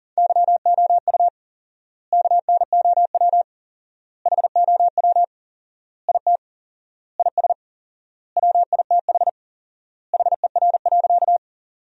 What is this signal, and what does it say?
Morse code, heard as a keyed tone, about 40 wpm, 700 Hz. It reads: YOU KNOW HOW IT IS WITH HER.
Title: 4x Word-Spacing - Sentences from Top 100 Words - 40wpm